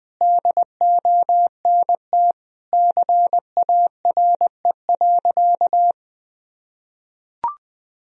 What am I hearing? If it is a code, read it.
Morse code, heard as a keyed tone, about 21 wpm, 700 Hz. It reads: DONT CARE.
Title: Sentences from Top 1000 Words - Review Part 1 - 20wpm